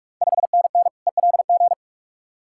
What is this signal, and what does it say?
Morse code, heard as a keyed tone, about 48 wpm, 690 Hz. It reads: 5NN ELG
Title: Ontario QSO Party Exchange 45wpm